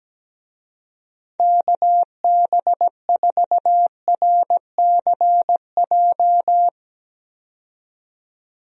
Morse code, heard as a keyed tone, about 17 wpm, 700 Hz. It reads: KB4RCJ
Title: USA Call Signs - 6 Characters - 17wpm